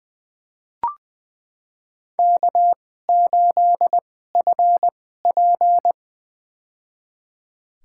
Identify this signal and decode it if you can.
Morse code, heard as a keyed tone, about 20 wpm, spaced as for 14 wpm, 700 Hz. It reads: K8FP
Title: USA Call Signs - 4 Characters - 2x Character Spacing - 20wpm